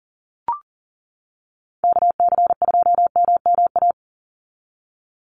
Morse code, heard as a keyed tone, about 39 wpm, 700 Hz. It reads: KC2MMA
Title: USA Call Signs 40wpm